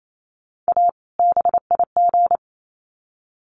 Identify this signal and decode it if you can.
Morse code, heard as a keyed tone, about 28 wpm, 700 Hz. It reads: A BIG
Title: Sets of 2 Words 28wpm